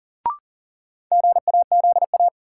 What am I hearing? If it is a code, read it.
Morse code, heard as a keyed tone, about 38 wpm, 700 Hz. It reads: GAZA
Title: Sets of 5 Words - New Word Drill - 40wpm